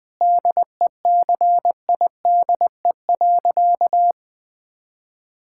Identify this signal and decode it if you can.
Morse code, heard as a keyed tone, about 21 wpm, 700 Hz. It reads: DECIDE.
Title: Sentences with 5-Word Phrases - Spoken Beginning - 20wpm